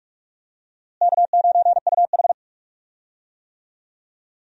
Morse code, heard as a keyed tone, about 43 wpm, 700 Hz. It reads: K9UH
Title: USA Call Signs - 4 Characters - Repeated 2x - 45wpm